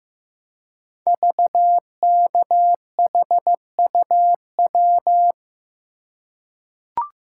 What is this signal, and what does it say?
Morse code, heard as a keyed tone, about 15 wpm, 700 Hz. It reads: VKHUW
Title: Code Groups - 5 Letters - 15wpm